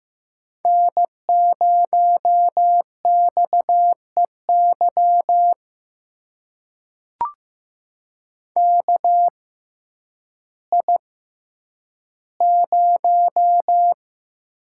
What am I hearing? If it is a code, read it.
Morse code, heard as a keyed tone, about 15 wpm, spaced as for 9 wpm, 700 Hz. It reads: N0XEY KI0
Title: USA Call Signs - 5 Characters - 6x Character Spacing - 15wpm